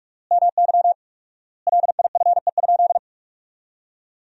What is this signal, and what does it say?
Morse code, heard as a keyed tone, about 42 wpm, 700 Hz. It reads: MY LIFE?
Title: 4x Word-Spacing - Sentences from Top 300 Words - 45wpm